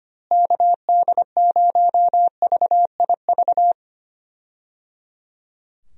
Morse code, heard as a keyed tone, about 25 wpm, 700 Hz. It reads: KD0VIV
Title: USA Call Signs - 6 Characters - Repeated 3x - 25wpm